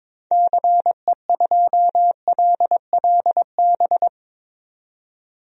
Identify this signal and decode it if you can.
Morse code, heard as a keyed tone, about 22 wpm, 700 Hz. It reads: CE2LLB
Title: DXCC-100 Call Signs - 6 Characters - 22wpm